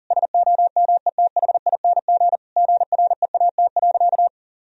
Morse code, heard as a keyed tone, about 37 wpm, 700 Hz. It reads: SOMETHING GREAT.
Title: Sentences from Top 1000 Words - Review Part 2 - 40wpm